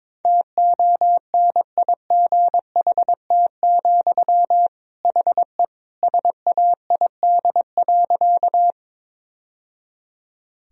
Morse code, heard as a keyed tone, about 22 wpm, 700 Hz. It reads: TONIGHT, HE SAID.
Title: Sentences from Top 1000 Words - Review Part 3 - 22wpm